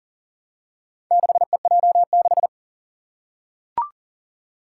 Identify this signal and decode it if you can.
Morse code, heard as a keyed tone, about 40 wpm, 700 Hz. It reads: 6EJ6
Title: Four Challenging Letter-Number Combinations 40wpm